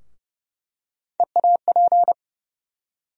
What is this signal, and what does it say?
Morse code, heard as a keyed tone, about 30 wpm, 700 Hz. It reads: EAP
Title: USA Call Sign Postfixes - 3 Letters 30wpm